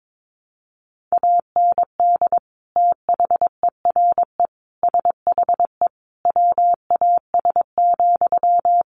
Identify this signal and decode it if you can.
Morse code, heard as a keyed tone, about 22 wpm, 700 Hz. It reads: AND THERE SHE WAS,
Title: Sentences from Top 1300 Words 22wpm